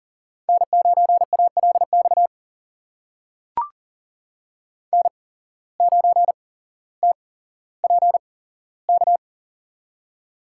Morse code, heard as a keyed tone, about 40 wpm, spaced as for 17 wpm, 700 Hz. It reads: N9ALX N9TPK
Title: USA Call Signs - 5 Characters - 8x Character Spacing - 40wpm